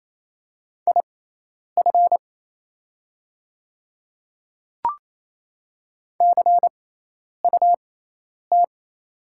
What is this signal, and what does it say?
Morse code, heard as a keyed tone, about 28 wpm, spaced as for 10 wpm, 700 Hz. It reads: IF CUT
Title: 6x Character Spacing - Top 300 Words - 28wpm